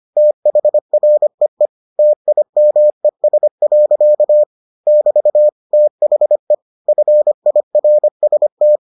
Morse code, heard as a keyed tone, about 25 wpm, 600 Hz. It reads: THREE TIMES. <BT> THE FIRST